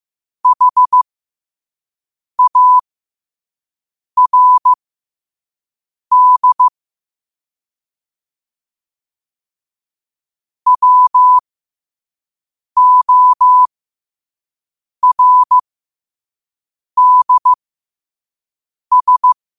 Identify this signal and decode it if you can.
Morse code, heard as a keyed tone, about 15 wpm, spaced as for 5 wpm, 1000 Hz. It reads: HARD WORDS